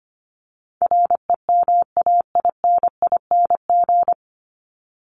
Morse code, heard as a keyed tone, about 25 wpm, 700 Hz. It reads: REMAINING
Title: ING Suffix - Top 100 Words with Base Words 25wpm